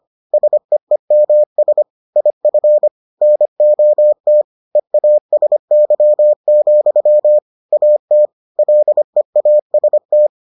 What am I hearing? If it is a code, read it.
Morse code, heard as a keyed tone, about 25 wpm, 600 Hz. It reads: SEEMS IF NOT EASY, AT LEAST